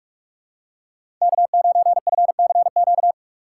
Morse code, heard as a keyed tone, about 46 wpm, 700 Hz. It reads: K9FCQ
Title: USA Call Signs - 5 Characters - 3x Character Spacing - 45wpm